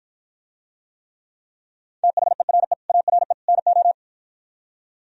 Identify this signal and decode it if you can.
Morse code, heard as a keyed tone, about 52 wpm, 700 Hz. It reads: THERE ARE NO